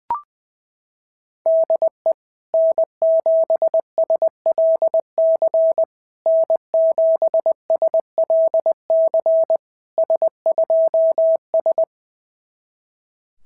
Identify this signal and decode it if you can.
Morse code, heard as a keyed tone, about 20 wpm, 650 Hz. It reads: DE N7SLC N7SLC S2S
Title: SOTA QSO Practice 20wpm